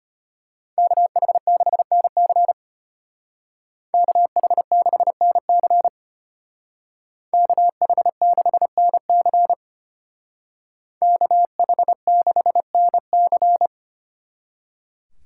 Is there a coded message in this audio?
Yes: Morse code from 0.8 to 13.7 s, heard as a keyed tone, about 32 wpm, 700 Hz. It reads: KH6NC KH6NC KH6NC KH6NC